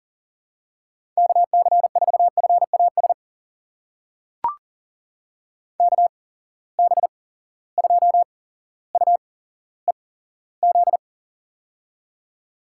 Morse code, heard as a keyed tone, about 38 wpm, spaced as for 17 wpm, 700 Hz. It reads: KC4FAS KB2UEZ
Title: USA Call Signs - 6 Characters - 8x Character Spacing - 40wpm